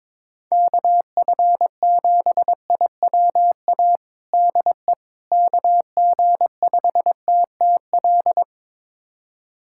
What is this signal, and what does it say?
Morse code, heard as a keyed tone, about 22 wpm, 700 Hz. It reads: KF7IWA DE KG5TTL
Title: USA Callsign DE Callsign 22wpm